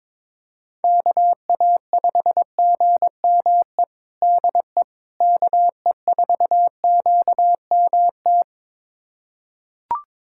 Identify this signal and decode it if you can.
Morse code, heard as a keyed tone, about 22 wpm, 700 Hz. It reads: KA5GME DE KE4QMT
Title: USA Callsign DE Callsign - 6 Characters - 22wpm